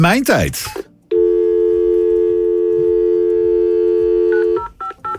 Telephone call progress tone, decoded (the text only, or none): dial tone